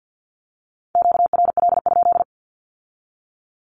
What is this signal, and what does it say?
Morse code, heard as a keyed tone, about 50 wpm, 700 Hz. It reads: QRL?
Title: Q-Codes with Spoken Meaning - 50wpm